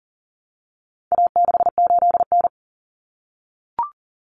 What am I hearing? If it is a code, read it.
Morse code, heard as a keyed tone, about 40 wpm, 700 Hz. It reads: A68N
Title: Common Cut Numbers - 4-Digit 40wpm